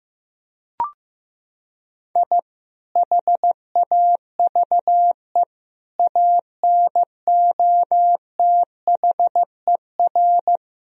Morse code, heard as a keyed tone, about 16 wpm, 700 Hz. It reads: I HAVE ANOTHER
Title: Sentences from Top 1000 Words - Review Part 1 - 15wpm